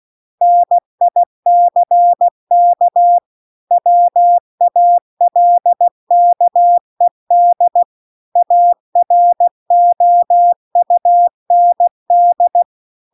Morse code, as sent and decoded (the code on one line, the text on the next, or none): -. .. -.-. -.- / .-- .- .-.. -.- . -.. / .- .-. --- ..- -. -..
NICK WALKED AROUND